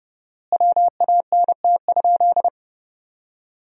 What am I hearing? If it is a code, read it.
Morse code, heard as a keyed tone, about 31 wpm, 700 Hz. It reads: WANT?
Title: 4x Word-Spacing - Sentences from Top 400 Words - 30wpm